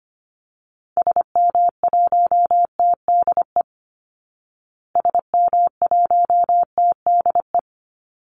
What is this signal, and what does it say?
Morse code, heard as a keyed tone, about 25 wpm, 700 Hz. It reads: SM1TDE SM1TDE
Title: Call Signs Repeated 3x - 25wpm